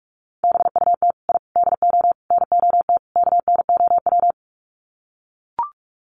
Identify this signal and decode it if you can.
Morse code, heard as a keyed tone, about 44 wpm, 700 Hz. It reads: BUT I DO NOT KNOW
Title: Sets of 5 Words 45wpm